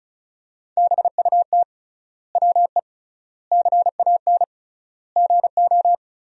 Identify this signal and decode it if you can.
Morse code, heard as a keyed tone, about 36 wpm, 700 Hz. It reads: BUT WE CAN GO
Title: 3x Word-Spacing - Sentences from Top 100 Words - 35wpm